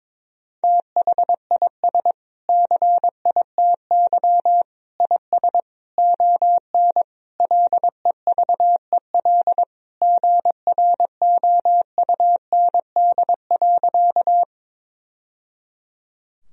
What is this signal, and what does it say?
Morse code, heard as a keyed tone, about 22 wpm, 700 Hz. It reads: THIS CITY IS ON LEVEL GROUND.
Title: Sentences from Top 1000 Words - Review Part 4 - 22wpm